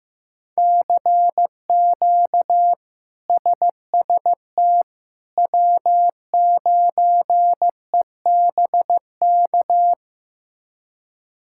Morse code, heard as a keyed tone, about 15 wpm, 700 Hz. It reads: CQ SST W9EBK
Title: K1USN SST QSO Practice - 15wpm